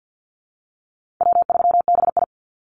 Morse code, heard as a keyed tone, about 50 wpm, 700 Hz. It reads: W3BI
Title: Easy Call Signs - 50wpm